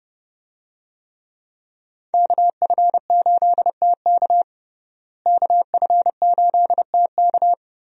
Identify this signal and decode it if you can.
Morse code, heard as a keyed tone, about 32 wpm, 700 Hz. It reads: KF8TK KF8TK
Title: USA Call Signs - 5 Characters - Repeated 3x - 30wpm